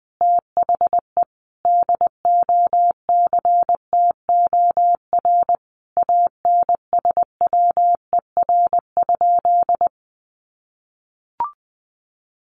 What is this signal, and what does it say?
Morse code, heard as a keyed tone, about 20 wpm, 700 Hz. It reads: THE DOCTOR ANSWER?
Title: Sentences from Top 1500 Words 20wpm